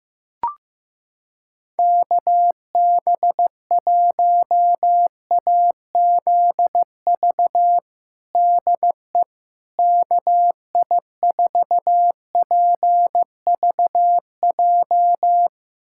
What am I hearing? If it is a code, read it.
Morse code, heard as a keyed tone, about 15 wpm, 700 Hz. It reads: KB1AZV DE KI4PVJ
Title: USA Callsign DE Callsign - 6 Characters - 15wpm